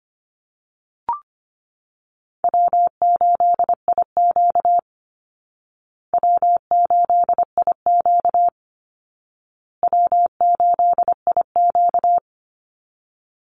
Morse code, heard as a keyed tone, about 25 wpm, 700 Hz. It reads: W8IQ W8IQ W8IQ
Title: Call Signs Repeated 2x - 25wpm